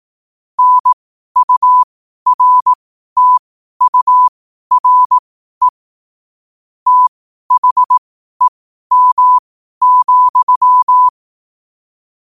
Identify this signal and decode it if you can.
Morse code, heard as a keyed tone, about 18 wpm, spaced as for 12 wpm, 1000 Hz. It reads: NURTURE THEM,
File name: qotdfiles / 12wpm